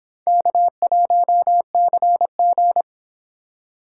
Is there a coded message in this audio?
Yes: Morse code from 0.3 to 2.8 s, heard as a keyed tone, about 26 wpm, 700 Hz. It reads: K1CG